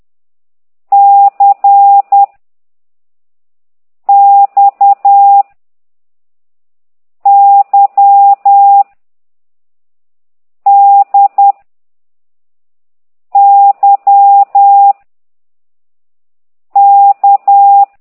Morse code, heard as a keyed tone, about 10 wpm, 800 Hz. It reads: C X Y D Y K
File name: ct2idw / exerc4